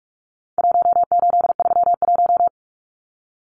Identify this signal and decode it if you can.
Morse code, heard as a keyed tone, about 44 wpm, 700 Hz. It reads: 1831
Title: Numbers - Contest Style - 4-Digit 45wpm